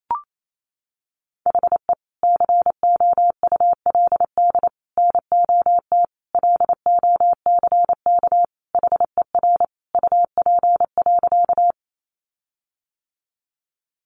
Morse code, heard as a keyed tone, about 28 wpm, 700 Hz. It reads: HE COULD NOT LOCK HER UP.
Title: Sentences from Top 1300 Words 28wpm